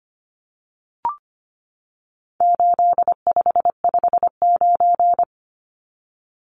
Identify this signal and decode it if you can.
Morse code, heard as a keyed tone, about 25 wpm, 700 Hz. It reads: 8559
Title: Numbers - 4-Digit 25wpm